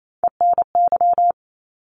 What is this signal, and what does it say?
Morse code, heard as a keyed tone, about 28 wpm, 700 Hz. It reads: ENY
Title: ARRL Sections 28wpm